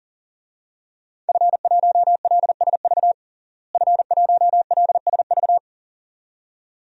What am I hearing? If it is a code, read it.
Morse code, heard as a keyed tone, about 40 wpm, 700 Hz. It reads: F1LSV F1LSV
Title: DXCC-100 Call Signs - 5 Characters - Repeated 2x - 40wpm